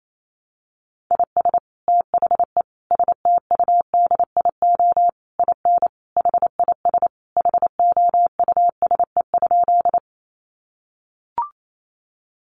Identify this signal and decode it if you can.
Morse code, heard as a keyed tone, about 28 wpm, 700 Hz. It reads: IS THE STUDIO IN HIS HOUSE?
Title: Sentences from Top 1200 Words 28wpm